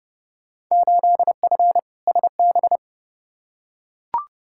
Morse code, heard as a keyed tone, about 30 wpm, 700 Hz. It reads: 8F SB
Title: Field Day Exchanges Encore - 30wpm